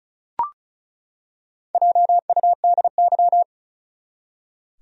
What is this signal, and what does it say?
Morse code, heard as a keyed tone, about 34 wpm, 700 Hz. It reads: JUDY
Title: USA Common Names - Max 4 Letters - 35wpm